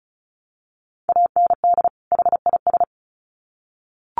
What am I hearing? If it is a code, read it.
Morse code, heard as a keyed tone, about 34 wpm, 700 Hz. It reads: AND HIS
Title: Sets of 2 Words 35wpm